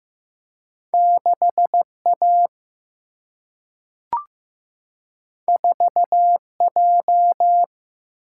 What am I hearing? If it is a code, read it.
Morse code, heard as a keyed tone, about 15 wpm, 700 Hz. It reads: 6A 4J
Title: Single-Letter Single-Number 15wpm